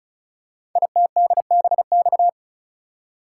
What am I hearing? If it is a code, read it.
Morse code, heard as a keyed tone, about 36 wpm, 690 Hz. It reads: ITDBX
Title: Code Groups - 5 Letters - 35wpm